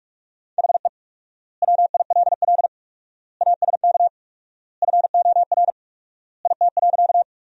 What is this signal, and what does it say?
Morse code, heard as a keyed tone, about 46 wpm, 700 Hz. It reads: HE WILL ASK FOR IT.
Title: 4x Word-Spacing - Sentences from Top 200 Words - 45wpm